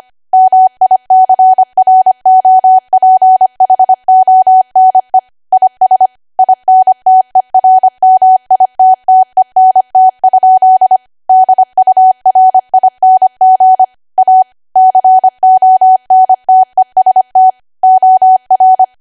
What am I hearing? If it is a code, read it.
Morse code, heard as a keyed tone, about 25 wpm, 750 Hz. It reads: MICROPHONE IS INTERMITTENT? DURING A CONTEST OR
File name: Archive / 25 WPM / 25WPM